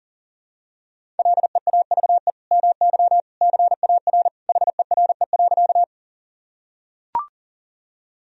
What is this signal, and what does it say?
Morse code, heard as a keyed tone, about 40 wpm, 700 Hz. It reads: LEAVE MY CAR HERE.
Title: Sentences from Top 1000 Words - Review Part 1 - 40wpm